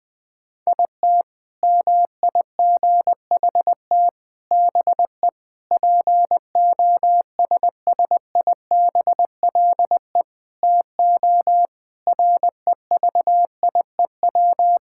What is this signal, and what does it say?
Morse code, heard as a keyed tone, about 20 wpm, 700 Hz. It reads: IT MIGHT BE POSSIBLE TO REVIEW
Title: Sentences from Top 1000 Words - Review Part 6 - 20wpm